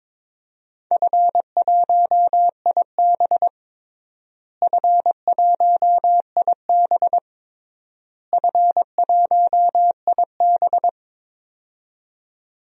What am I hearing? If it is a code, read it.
Morse code, heard as a keyed tone, about 22 wpm, 700 Hz. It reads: F1IB F1IB F1IB
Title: DXCC-100 Call Signs - 4 Characters - Repeated 3x - 22wpm